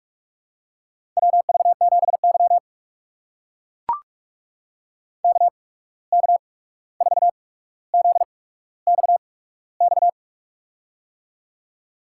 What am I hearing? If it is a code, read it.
Morse code, heard as a keyed tone, about 44 wpm, spaced as for 17 wpm, 700 Hz. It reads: WV7Y KK4ZXX
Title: USA Call Signs - 8x Character Spacing - 45wpm